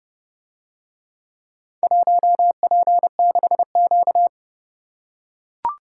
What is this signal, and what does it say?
Morse code, heard as a keyed tone, about 30 wpm, 700 Hz. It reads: 1P6Q